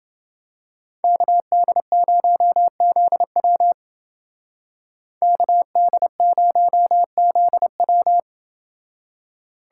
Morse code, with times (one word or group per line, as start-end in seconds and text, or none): 1.0-3.7 KD0ZW
5.2-8.2 KD0ZW